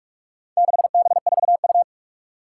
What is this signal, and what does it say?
Morse code, heard as a keyed tone, about 46 wpm, 700 Hz. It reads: 6B4U
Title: Four Challenging Letter-Number Combinations 45wpm